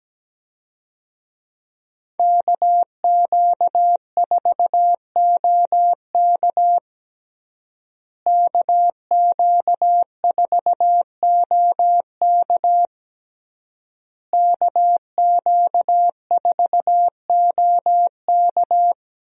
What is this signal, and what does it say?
Morse code, heard as a keyed tone, about 17 wpm, 700 Hz. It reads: KQ4OK KQ4OK KQ4OK
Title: USA Call Signs - 5 Characters - Repeated 3x - 17wpm